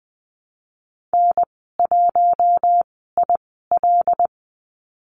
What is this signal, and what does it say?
Morse code, heard as a keyed tone, about 19 wpm, spaced as for 14 wpm, 700 Hz. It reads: N1IL
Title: 147 Lesson 24g - Letter y - Call Signs 20wpm